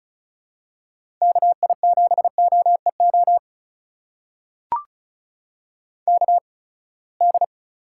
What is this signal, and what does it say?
Morse code, heard as a keyed tone, about 36 wpm, spaced as for 24 wpm, 700 Hz. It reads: KI7OEO KD
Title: USA Call Signs - 6 Characters - 8x Character Spacing - 35wpm